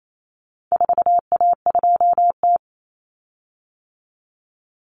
Morse code, heard as a keyed tone, about 28 wpm, 700 Hz. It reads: 4A2T
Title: Common Cut Numbers - 4-Digit 28wpm